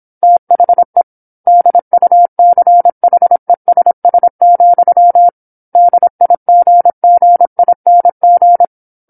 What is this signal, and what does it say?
Morse code, heard as a keyed tone, about 27 wpm, 700 Hz. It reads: THE DUCHESS, DIGGING
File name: 08-Chapter-08 26 WPM